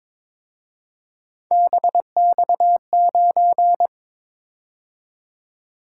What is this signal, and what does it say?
Morse code, heard as a keyed tone, about 26 wpm, 700 Hz. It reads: BX9
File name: Speed-Racing - Three Letter-Number Combinations - 22wpm